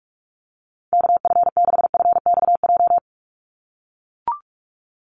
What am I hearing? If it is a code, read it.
Morse code, heard as a keyed tone, about 44 wpm, 700 Hz. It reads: KF6FXJ
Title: USA Call Signs 45wpm